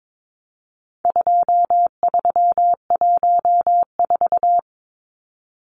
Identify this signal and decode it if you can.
Morse code, heard as a keyed tone, about 22 wpm, 700 Hz. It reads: 2314
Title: Numbers - 4-Digit 22wpm